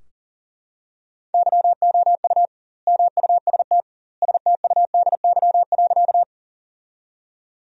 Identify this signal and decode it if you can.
Morse code, heard as a keyed tone, about 40 wpm, 700 Hz. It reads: YOU MUST STUDY.
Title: 2x Word-Spacing - Sentences from Top 300 Words - 40wpm